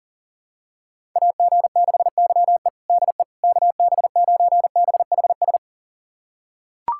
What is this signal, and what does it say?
Morse code, heard as a keyed tone, about 40 wpm, 700 Hz. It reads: AG6YE DE KB9BHS
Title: USA Callsign DE Callsign 40wpm